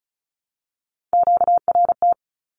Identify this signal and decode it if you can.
Morse code, heard as a keyed tone, about 34 wpm, 700 Hz. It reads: QRT
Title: Q-Codes Repeated 3x 35wpm